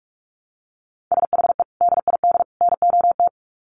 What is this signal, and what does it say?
Morse code, heard as a keyed tone, about 44 wpm, 700 Hz. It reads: SHE DID NOT
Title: Sets of 3 Words 45wpm